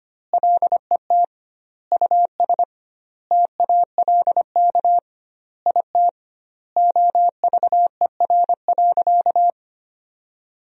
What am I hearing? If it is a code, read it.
Morse code, heard as a keyed tone, about 25 wpm, 700 Hz. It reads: LET US TALK IT OVER.
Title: 2x Word-Spacing - Sentences from Top 500 Words - 25wpm